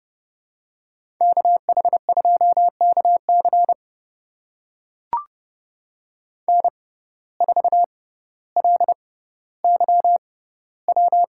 Morse code, heard as a keyed tone, about 30 wpm, spaced as for 15 wpm, 700 Hz. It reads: KH2KC N4LYW
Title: USA Call Signs - 5 Characters - 6x Character Spacing - 30wpm